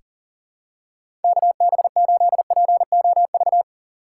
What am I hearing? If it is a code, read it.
Morse code, heard as a keyed tone, about 40 wpm, 700 Hz. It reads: KB8POV
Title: USA Call Signs - 6 Characters - 3x Character Spacing - 40wpm